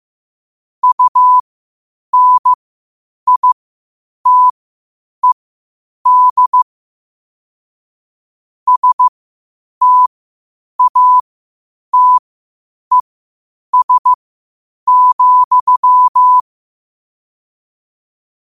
Morse code, heard as a keyed tone, about 15 wpm, spaced as for 8 wpm, 1000 Hz. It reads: UNITED STATES,
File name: qotdfiles / 8wpm